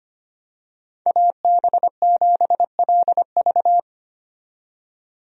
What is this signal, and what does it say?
Morse code, heard as a keyed tone, about 25 wpm, 700 Hz. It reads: AB7LV
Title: USA Call Signs - 5 Characters - 25wpm